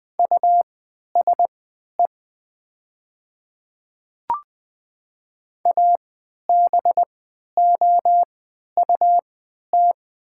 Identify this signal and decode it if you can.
Morse code, heard as a keyed tone, about 20 wpm, spaced as for 11 wpm, 700 Hz. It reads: USE ABOUT
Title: 3x Character Spacing - Top 100 Words - 20wpm